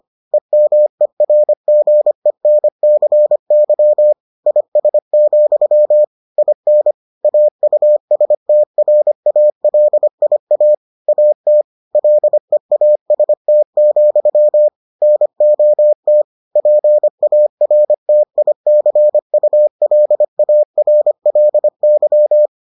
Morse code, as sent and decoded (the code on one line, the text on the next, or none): . -- . .-. --. -. -.-. -.-- / .. ... --..-- / .. -. / .- ..- ... - .-. .- .-.. .. .- / .- - / .-.. .- ... - --..-- / -. --- - / .--. .- .-. - .. -.-. ..- .-.. .- .-. .-.. -.--
EMERGENCY IS, IN AUSTRALIA AT LEAST, NOT PARTICULARLY